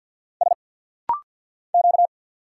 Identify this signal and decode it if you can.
Morse code, heard as a keyed tone, about 50 wpm, 700 Hz. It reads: S Q
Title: Single Letter-Number Alt - Rapid-Fire 50wpm